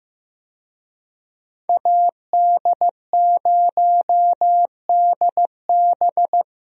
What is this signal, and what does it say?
Morse code, heard as a keyed tone, about 15 wpm, 700 Hz. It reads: AD0DB
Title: USA Call Signs - 5 Characters - Repeated 2x - 15wpm